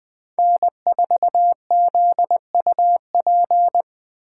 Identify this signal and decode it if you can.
Morse code, heard as a keyed tone, about 20 wpm, 700 Hz. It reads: N4ZUP